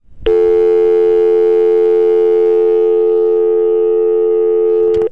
Dial tone (North America), 350 + 440 Hz.